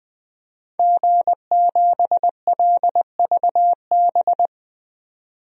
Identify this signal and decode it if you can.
Morse code, heard as a keyed tone, about 20 wpm, 700 Hz. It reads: G7LVB
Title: DXCC-100 Call Signs - 5 Characters - 20wpm